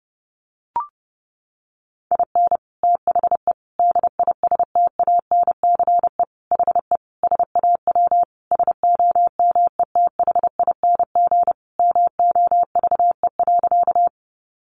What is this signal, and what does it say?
Morse code, heard as a keyed tone, about 30 wpm, 700 Hz. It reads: IN THE DISTANCE HE SAW SOMETHING MOVE.
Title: Sentences from Top 1200 Words 30wpm